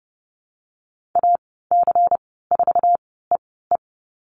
Morse code, heard as a keyed tone, about 30 wpm, spaced as for 17 wpm, 700 Hz. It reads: AC4EE
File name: Lesson 19g - Letter w - Call Signs 30wpm